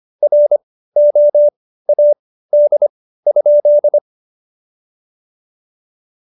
Morse code, heard as a keyed tone, about 25 wpm, spaced as for 15 wpm, 590 Hz. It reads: ROAD?
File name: Jokes-25-char-15-eff